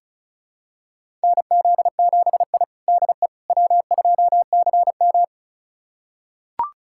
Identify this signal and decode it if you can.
Morse code, heard as a keyed tone, about 36 wpm, 700 Hz. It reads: NZ7I DE W2CM
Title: USA Callsign DE Callsign - 4 Characters - 35wpm